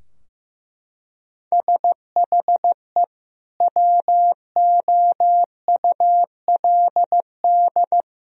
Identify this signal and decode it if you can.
Morse code, heard as a keyed tone, about 15 wpm, 700 Hz. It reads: SHE WOULD
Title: Sentences from Top 1000 Words - Review Part 3 - 15wpm